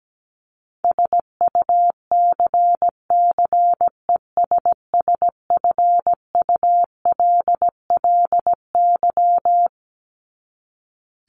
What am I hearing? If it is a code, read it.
Morse code, heard as a keyed tone, about 17 wpm, 700 Hz. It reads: SUCCESSFULLY
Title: LY Suffix - Top 100 Words with Base Words 17wpm